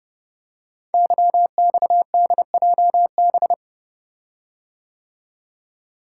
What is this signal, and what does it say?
Morse code, heard as a keyed tone, about 30 wpm, 700 Hz. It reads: YXDJB
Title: Code Groups - 5 Letters - 30wpm